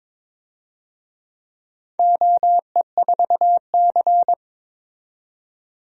Morse code, heard as a keyed tone, about 22 wpm, 700 Hz. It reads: OE4C
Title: DXCC-100 Call Signs - 4 Characters - 22wpm